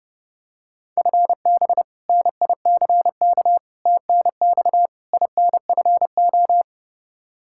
Morse code, heard as a keyed tone, about 32 wpm, 700 Hz. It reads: FB NICK TNX INFO
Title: Practice QSO - Exchange 3 - More Info - 30wpm